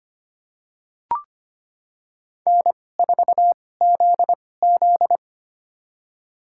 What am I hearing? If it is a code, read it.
Morse code, heard as a keyed tone, about 25 wpm, spaced as for 18 wpm, 700 Hz. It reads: N4ZZ
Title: USA Call Signs - 4 Characters - 2x Character Spacing - 25wpm